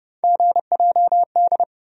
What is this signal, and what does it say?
Morse code, heard as a keyed tone, about 29 wpm, 700 Hz. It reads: GJD
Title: USA Call Sign Postfixes - 3 Letters 30wpm